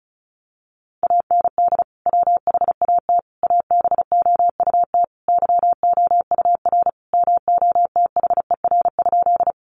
Morse code, heard as a keyed tone, about 34 wpm, 700 Hz. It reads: AND WHAT ABOUT YOUR MOTHER?
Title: Sentences from Top 200 Words 35wpm